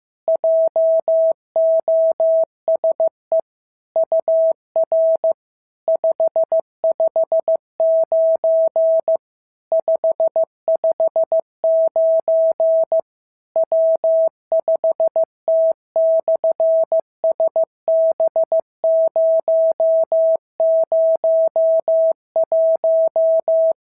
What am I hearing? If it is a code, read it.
Morse code, heard as a keyed tone, about 15 wpm, 650 Hz. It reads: JOSE UR 559 559 W5T/SB001
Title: SOTA QSO Practice 15wpm